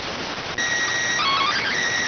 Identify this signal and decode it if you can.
SSTV (slow-scan television), PD 290.